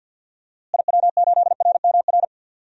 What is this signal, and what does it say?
Morse code, heard as a keyed tone, about 51 wpm, 700 Hz. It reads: IW8RMR
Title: DXCC-100 Call Signs - 6 Characters - Repeated 3x - 50wpm